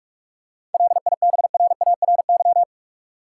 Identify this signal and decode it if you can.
Morse code, heard as a keyed tone, about 46 wpm, 700 Hz. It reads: LIBRARY
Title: Seven Letter Words - 45wpm